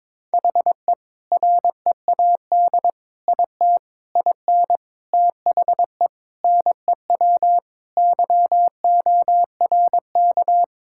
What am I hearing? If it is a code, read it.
Morse code, heard as a keyed tone, about 22 wpm, 700 Hz. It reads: HE READ IT IN THE NEW YORK
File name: practice / 22wpm / Sentences with 4-Word Phrases - 22wpm